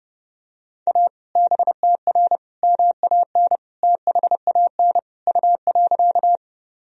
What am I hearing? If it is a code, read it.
Morse code, heard as a keyed tone, about 30 wpm, 700 Hz. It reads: A BTR MAN THAN U.
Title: Sentences from Top 400 Words and Abbreviations 30wpm